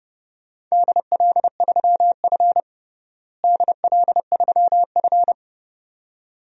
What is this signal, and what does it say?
Morse code, heard as a keyed tone, about 30 wpm, 700 Hz. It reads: DL3F DL3F
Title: DXCC-100 Call Signs - 4 Characters - Repeated 2x - 30wpm